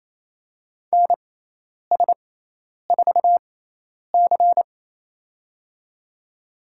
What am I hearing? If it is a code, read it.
Morse code, heard as a keyed tone, about 28 wpm, spaced as for 10 wpm, 700 Hz. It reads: NS4C
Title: USA Call Signs - 4 Characters - 6x Character Spacing - 28wpm